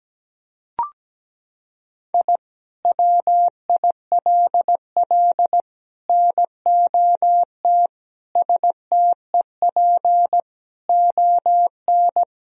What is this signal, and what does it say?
Morse code, heard as a keyed tone, about 17 wpm, 700 Hz. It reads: I WILL NOT STEP ON
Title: Sentences from Top 1000 Words - Review Part 2 - 17wpm